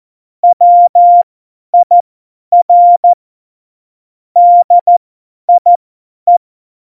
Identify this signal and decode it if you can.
Morse code, heard as a keyed tone, about 14 wpm, spaced as for 10 wpm, 700 Hz. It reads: WIR DIE
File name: Morsekurs / Witze / Witz 26. September 17, Tempo 50